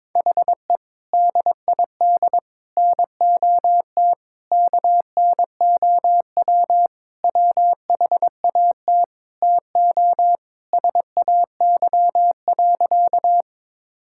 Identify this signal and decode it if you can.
Morse code, heard as a keyed tone, about 22 wpm, 700 Hz. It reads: HE DID NOT KNOW WHAT TO SAY.
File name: practice / 22wpm / Sentences with 3-Word Phrases - 22wpm